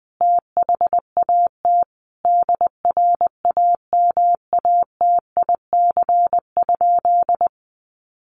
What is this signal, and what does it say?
Morse code, heard as a keyed tone, about 20 wpm, 700 Hz. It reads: THAT DRAMATIC?
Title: Sentences from Top 1300 Words 20wpm